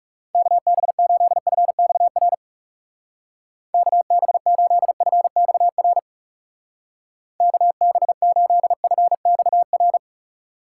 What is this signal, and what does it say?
Morse code, heard as a keyed tone, about 37 wpm, 700 Hz. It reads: KB8FXR KB8FXR KB8FXR
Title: Speed Racing - USA 2x3 Call Signs 30wpm